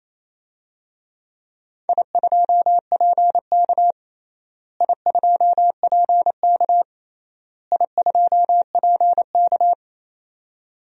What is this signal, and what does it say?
Morse code, heard as a keyed tone, about 28 wpm, 700 Hz. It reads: I2PK I2PK I2PK
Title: DXCC-100 Call Signs - 4 Characters - Repeated 3x - 28wpm